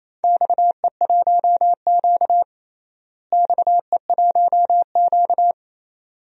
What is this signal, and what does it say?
Morse code, heard as a keyed tone, about 28 wpm, 700 Hz. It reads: XE1Q XE1Q
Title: DXCC-100 Call Signs - 4 Characters - Repeated 2x - 28wpm